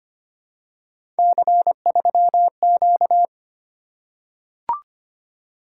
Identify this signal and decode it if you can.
Morse code, heard as a keyed tone, about 25 wpm, 700 Hz. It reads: C3Q
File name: Three Letter-Number Combinations - 25wpm